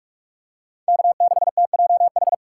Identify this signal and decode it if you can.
Morse code, heard as a keyed tone, about 43 wpm, 700 Hz. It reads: K6TJH
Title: USA Call Signs - 5 Characters - 4x Character Spacing - 45wpm